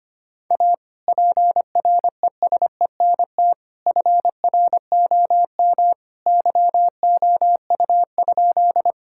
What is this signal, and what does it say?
Morse code, heard as a keyed tone, about 25 wpm, 700 Hz. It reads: A PRESENT FROM YOU?
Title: Sentences from Top 1000 Words - Review Part 5 - 25wpm